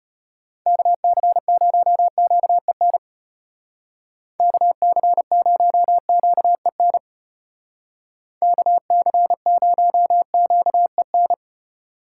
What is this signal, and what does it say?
Morse code, heard as a keyed tone, about 31 wpm, 700 Hz. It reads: KC0QEN KC0QEN KC0QEN
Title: Speed Racing - USA Call Signs 25wpm